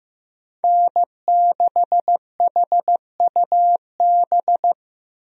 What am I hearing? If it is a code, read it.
Morse code, heard as a keyed tone, about 16 wpm, 700 Hz. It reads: N6HUB
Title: USA Call Signs - 5 Characters - Repeated 3x - 15wpm